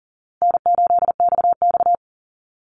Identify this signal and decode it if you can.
Morse code, heard as a keyed tone, about 40 wpm, 700 Hz. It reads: N8XX
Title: Call Signs Repeated 2x - 40wpm